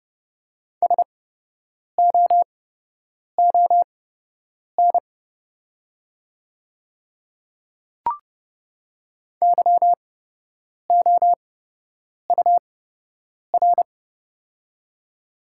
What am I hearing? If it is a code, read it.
Morse code, heard as a keyed tone, about 30 wpm, spaced as for 8 wpm, 700 Hz. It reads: SOON YOUR
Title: 8x Character Spacing - Top 500 Words - 30wpm